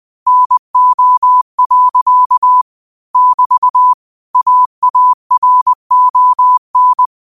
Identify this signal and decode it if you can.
Morse code, heard as a keyed tone, about 20 wpm, 1000 Hz. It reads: NO. <BT> AARON
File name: qotdfiles / 20wpm